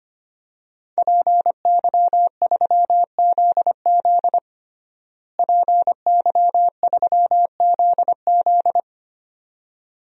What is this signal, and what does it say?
Morse code, heard as a keyed tone, about 25 wpm, 700 Hz. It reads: PY3ZZ PY3ZZ
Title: DXCC-100 Call Signs - 5 Characters - Repeated 2x - 25wpm